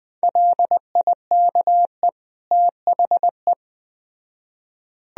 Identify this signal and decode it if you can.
Morse code, heard as a keyed tone, about 21 wpm, 700 Hz. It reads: LIKE THE